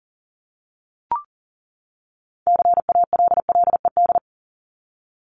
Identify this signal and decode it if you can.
Morse code, heard as a keyed tone, about 40 wpm, 700 Hz. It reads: CALLED
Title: ED Suffix - Top 100 Words with Base Words 40wpm